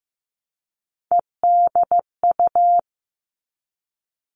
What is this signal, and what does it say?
Morse code, heard as a keyed tone, about 15 wpm, 700 Hz. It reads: EDU
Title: All Cut Numbers - 3-Digit 15wpm